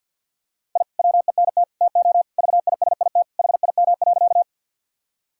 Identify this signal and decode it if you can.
Morse code, heard as a keyed tone, about 50 wpm, 700 Hz. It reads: I WENT TO VISIT HIM.
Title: Sentences from Top 1000 Words - Review Part 6 - 50wpm